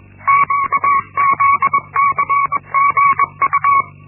Morse code, heard as a keyed tone, about 21 wpm, 1120 Hz. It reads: QZCGU
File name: Linksmakalnis-1